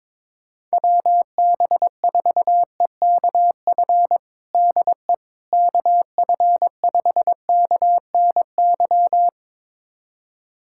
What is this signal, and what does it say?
Morse code, heard as a keyed tone, about 22 wpm, 700 Hz. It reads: WB4EKF DE KF5KNY
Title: USA Callsign DE Callsign - 6 Characters - 22wpm